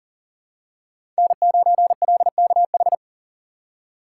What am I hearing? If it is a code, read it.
Morse code, heard as a keyed tone, about 39 wpm, 700 Hz. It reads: N9LKH
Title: USA Call Signs - 5 Characters - 6x Character Spacing - 40wpm